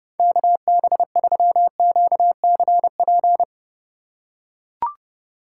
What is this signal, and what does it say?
Morse code, heard as a keyed tone, about 29 wpm, 700 Hz. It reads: KB3QCP